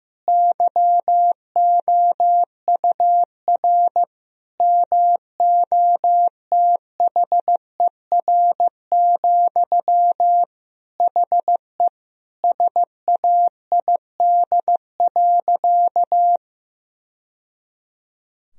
Morse code, heard as a keyed tone, about 15 wpm, 700 Hz. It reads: YOUR MOTHER, HE SAID.
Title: Sentences from Top 1000 Words - Review Part 4 - 15wpm